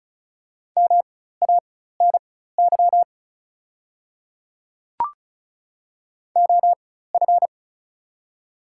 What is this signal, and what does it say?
Morse code, heard as a keyed tone, about 36 wpm, spaced as for 16 wpm, 690 Hz. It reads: MANY OF
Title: 4x Character Spacing - Top 100 Words - 35wpm